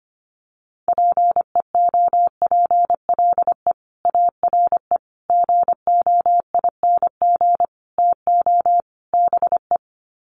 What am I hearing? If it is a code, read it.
Morse code, heard as a keyed tone, about 25 wpm, 700 Hz. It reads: PEOPLE ARE GOING TO BE
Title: Sets of 5 Words Encore 25wpm